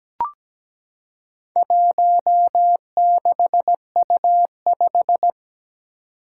Morse code, heard as a keyed tone, about 17 wpm, 700 Hz. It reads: 16U5
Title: Four Challenging Letter-Number Combinations 17wpm